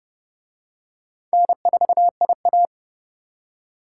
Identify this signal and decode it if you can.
Morse code, heard as a keyed tone, about 27 wpm, 700 Hz. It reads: N4IA